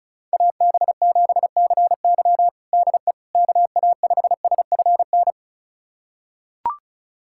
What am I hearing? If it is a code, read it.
Morse code, heard as a keyed tone, about 36 wpm, 700 Hz. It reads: AB7CY DE KA5SFN
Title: USA Callsign DE Callsign 35wpm